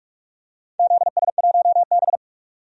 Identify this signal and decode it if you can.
Morse code, heard as a keyed tone, about 46 wpm, 700 Hz. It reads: ZS1B
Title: DXCC-100 Call Signs - 4 Characters - 45wpm